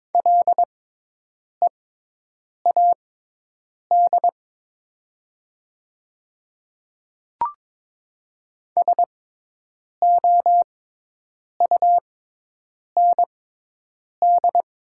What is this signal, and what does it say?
Morse code, heard as a keyed tone, about 22 wpm, spaced as for 8 wpm, 700 Hz. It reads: LEAD SOUND
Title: 6x Character Spacing - Top 500 Words - 22wpm